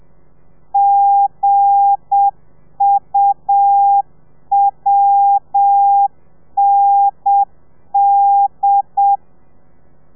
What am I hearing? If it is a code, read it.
Morse code, heard as a keyed tone, about 7 wpm, 800 Hz. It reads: GUWND
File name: telegrafia / i2tfj / Lez8 35